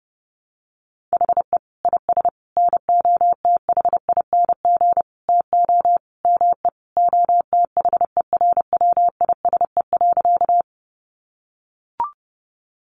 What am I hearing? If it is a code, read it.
Morse code, heard as a keyed tone, about 30 wpm, 700 Hz. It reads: HE IS NOTHING TO ME OTHERWISE.
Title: Sentences from Top 1200 Words 30wpm